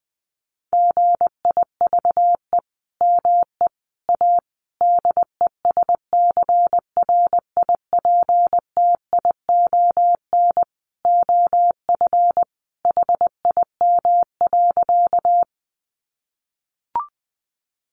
Morse code, heard as a keyed tone, about 20 wpm, 700 Hz. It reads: GIVE ME A DESCRIPTION OF HIM.
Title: Sentences from Top 1300 Words 20wpm